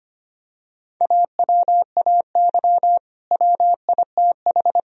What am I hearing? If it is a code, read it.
Morse code, heard as a keyed tone, about 25 wpm, 700 Hz. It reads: AWAY WITH
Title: Sentences from Top 1000 Words - Review Part 4 - 25wpm